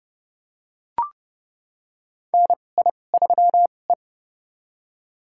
Morse code, heard as a keyed tone, about 29 wpm, spaced as for 21 wpm, 700 Hz. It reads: NI3E
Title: USA Call Signs - 4 Characters - 2x Character Spacing - 30wpm